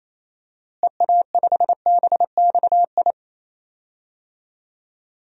Morse code, heard as a keyed tone, about 28 wpm, 700 Hz. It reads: EA5BXI
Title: DXCC-100 Call Signs - 6 Characters - Repeated 3x - 28wpm